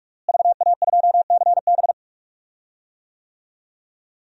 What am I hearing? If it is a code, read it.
Morse code, heard as a keyed tone, about 46 wpm, 700 Hz. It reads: VA2CB
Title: DXCC-100 Call Signs - 5 Characters - Repeated 2x - 45wpm